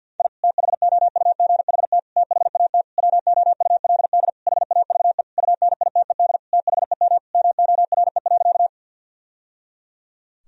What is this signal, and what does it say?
Morse code, heard as a keyed tone, about 49 wpm, 700 Hz. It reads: I THOUGHT THAT WOULD HAVE UNITED THEM MORE.
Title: Sentences from Top 1000 Words - Review Part 4 - 50wpm